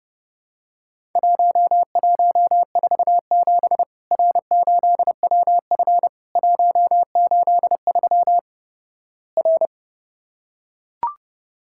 Morse code, heard as a keyed tone, about 30 wpm, 700 Hz. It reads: 1147 R8WF 183 R
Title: Worked All Europe DX Contest - QTC Exchange - 30wpm